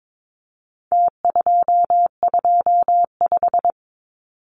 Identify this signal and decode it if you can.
Morse code, heard as a keyed tone, about 22 wpm, 700 Hz. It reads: T225
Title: Numbers - Contest Style - 4-Digit 22wpm